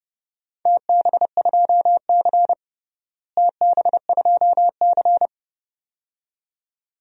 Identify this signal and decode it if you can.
Morse code, heard as a keyed tone, about 30 wpm, 700 Hz. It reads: TB2C TB2C
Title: DXCC-100 Call Signs - 4 Characters - Repeated 2x - 30wpm